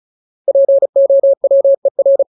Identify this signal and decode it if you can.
Morse code, heard as a keyed tone, about 34 wpm, 550 Hz. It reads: POWER
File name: Ultimate Word List / 35WPM Ultimate